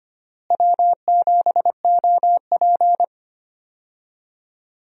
Morse code, heard as a keyed tone, about 25 wpm, 700 Hz. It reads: W7OP
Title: USA Call Signs - 4 Characters - Repeated 2x - 25wpm